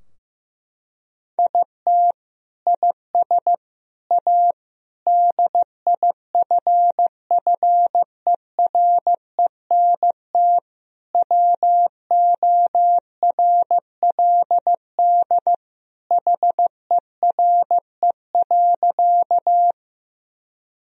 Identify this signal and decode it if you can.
Morse code, heard as a keyed tone, about 15 wpm, 700 Hz. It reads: IT IS A DIFFERENT WORLD HERE.
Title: Sentences from Top 1000 Words - Review Part 3 - 15wpm